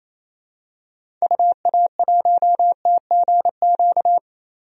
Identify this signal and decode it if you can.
Morse code, heard as a keyed tone, about 28 wpm, 700 Hz. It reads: UA1TGQ
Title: DXCC-100 Call Signs - 6 Characters - Repeated 2x - 28wpm